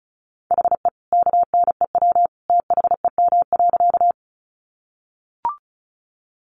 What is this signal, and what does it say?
Morse code, heard as a keyed tone, about 34 wpm, 700 Hz. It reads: HE KNEW THEM.
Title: Sentences from Top 400 Words 35wpm